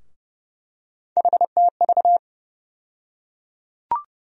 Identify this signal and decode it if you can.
Morse code, heard as a keyed tone, about 31 wpm, 700 Hz. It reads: HTV
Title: USA Call Sign Postfixes - 3 Letters 30wpm